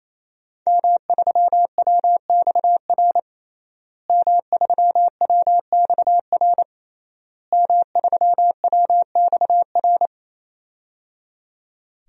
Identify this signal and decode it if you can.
Morse code, heard as a keyed tone, about 27 wpm, 700 Hz. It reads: M3WXR M3WXR M3WXR
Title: DXCC-100 Call Signs - 5 Characters - Repeated 3x - 28wpm